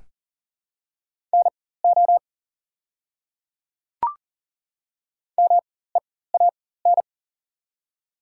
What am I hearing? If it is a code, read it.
Morse code, heard as a keyed tone, about 40 wpm, spaced as for 19 wpm, 690 Hz. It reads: NO MEAN
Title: 4x Character Spacing - Top 300 Words - 40wpm